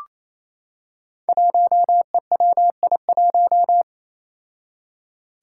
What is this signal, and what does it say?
Morse code, heard as a keyed tone, about 29 wpm, 700 Hz. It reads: 1EWI1
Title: Five Challenging Letter-Number Combinations 28wpm